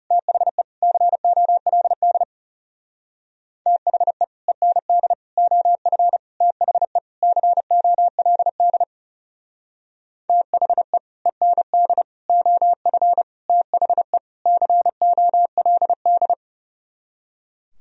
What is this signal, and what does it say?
Morse code, heard as a keyed tone, about 37 wpm, 700 Hz. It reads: THE COLD THE END OF THE COLD THE END OF THE COLD
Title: Speed-Racing - Sets of 5 Words 30wpm